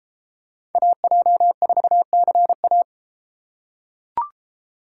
Morse code, heard as a keyed tone, about 31 wpm, 700 Hz. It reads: AJ4CA